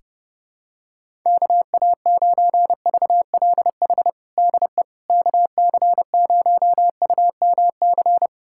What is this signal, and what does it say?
Morse code, heard as a keyed tone, about 32 wpm, 700 Hz. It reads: KA9VLH DE KC0UMC
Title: USA Callsign DE Callsign - 6 Characters - 30wpm